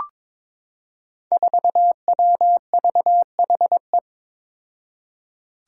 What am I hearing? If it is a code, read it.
Morse code, heard as a keyed tone, about 23 wpm, 700 Hz. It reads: 4WVHE
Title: Five Challenging Letter-Number Combinations 22wpm